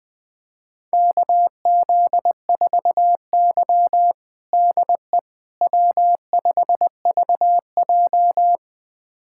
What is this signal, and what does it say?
Morse code, heard as a keyed tone, about 21 wpm, 700 Hz. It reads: KZ4Y DE W5VJ
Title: USA Callsign DE Callsign - 4 Characters - 20wpm